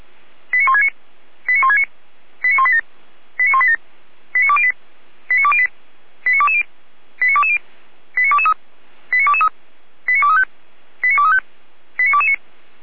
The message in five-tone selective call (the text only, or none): 89169 89169 89187 89187 89208 89208 89220 89220 89223 89223 89235 89235 89229